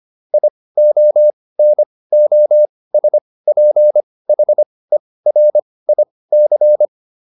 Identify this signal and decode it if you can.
Morse code, heard as a keyed tone, about 25 wpm, spaced as for 18 wpm, 600 Hz. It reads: IONOSPHERIC
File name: SS211 18